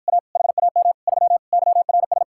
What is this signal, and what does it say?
Morse code, heard as a keyed tone, about 52 wpm, 700 Hz. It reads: A HAM 3 YRS